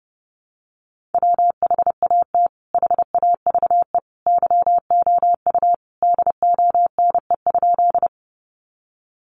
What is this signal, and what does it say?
Morse code, heard as a keyed tone, about 30 wpm, 700 Hz. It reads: WHAT HAVE YOU DONE?
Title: Sentences from Top 500 Words 30wpm